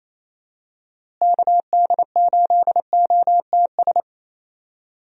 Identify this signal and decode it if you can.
Morse code, heard as a keyed tone, about 28 wpm, 700 Hz. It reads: KD8OTS
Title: USA Call Signs - 6 Characters - Repeated 3x - 28wpm